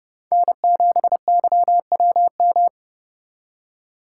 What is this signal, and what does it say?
Morse code, heard as a keyed tone, about 31 wpm, 700 Hz. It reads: N7YWM
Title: USA Call Signs - 4x Character Spacing - 30wpm